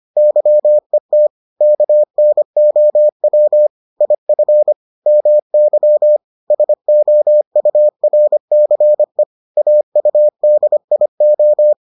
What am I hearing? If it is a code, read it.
Morse code, heard as a keyed tone, about 25 wpm, 600 Hz. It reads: YET KNOW IF MY SOURCE AUDIO